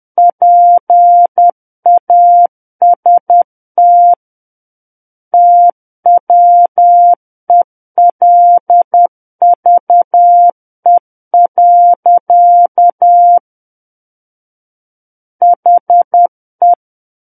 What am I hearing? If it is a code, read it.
Morse code, heard as a keyed tone, about 10 wpm, 700 Hz. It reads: PAST TWELVE. HE
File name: Book One / 01-Chapter 1 - 10 WPM